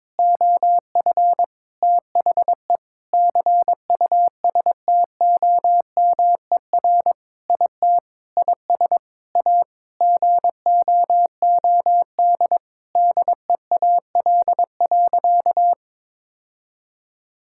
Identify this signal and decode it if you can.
Morse code, heard as a keyed tone, about 22 wpm, 700 Hz. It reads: OF THE CUSTOMER IT IS A GOOD DEAL.
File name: Sentences with 7-Word Phrases - 22wpm